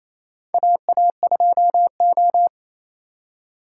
Morse code, heard as a keyed tone, about 28 wpm, 700 Hz. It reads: AA2O
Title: USA Call Signs - 3x Character Spacing - 28wpm